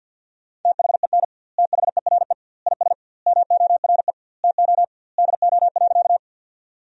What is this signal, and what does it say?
Morse code, heard as a keyed tone, about 50 wpm, 700 Hz. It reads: THEN THERE IS MORE TO DO.
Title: 2x Word-Spacing - Sentences from Top 100 Words - 50wpm